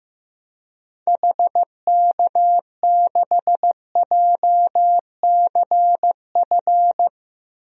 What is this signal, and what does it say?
Morse code, heard as a keyed tone, about 15 wpm, 700 Hz. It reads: HK6JCF
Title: DXCC-100 Call Signs - 6 Characters - Repeated 2x - 15wpm